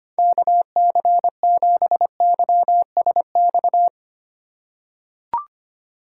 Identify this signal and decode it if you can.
Morse code, heard as a keyed tone, about 25 wpm, 700 Hz. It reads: KC7YSX
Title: USA Call Signs - 6 Characters - 3x Character Spacing - 25wpm